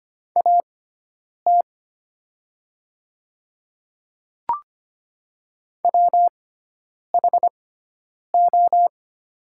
Morse code, heard as a keyed tone, about 25 wpm, spaced as for 9 wpm, 700 Hz. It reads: AT WHO
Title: 6x Character Spacing - Top 100 Words - 25wpm